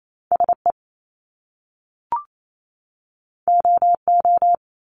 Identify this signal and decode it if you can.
Morse code, heard as a keyed tone, about 28 wpm, 700 Hz. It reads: SE OO
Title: Top Two Letter Combinations in Words 28wpm